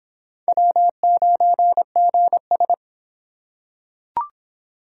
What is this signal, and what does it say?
Morse code, heard as a keyed tone, about 23 wpm, 700 Hz. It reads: W9GS